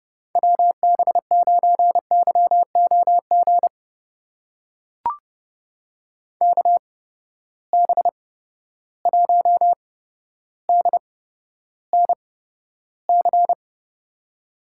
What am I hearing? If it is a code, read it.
Morse code, heard as a keyed tone, about 30 wpm, spaced as for 13 wpm, 700 Hz. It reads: WB9YOG KB1DNC
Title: USA Call Signs - 6 Characters - 8x Character Spacing - 30wpm